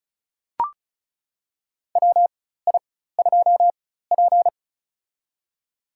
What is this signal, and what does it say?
Morse code, heard as a keyed tone, about 36 wpm, spaced as for 16 wpm, 700 Hz. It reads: WI2P
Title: USA Call Signs - 4x Character Spacing - 35wpm